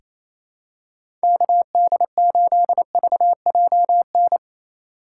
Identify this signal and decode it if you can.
Morse code, heard as a keyed tone, about 27 wpm, 700 Hz. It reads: KD8VJN
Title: USA Call Signs - 3x Character Spacing - 28wpm